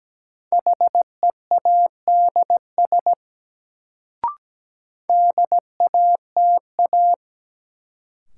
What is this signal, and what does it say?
Morse code, heard as a keyed tone, about 17 wpm, 700 Hz. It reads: HEADS DATA
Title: Sentences from Top 1000 Words - Review Part 3 - New Word Drill - 17wpm